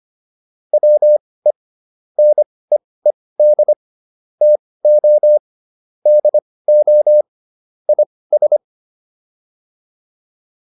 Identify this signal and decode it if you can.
Morse code, heard as a keyed tone, about 25 wpm, spaced as for 18 wpm, 600 Hz. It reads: WE NEED TO DO IS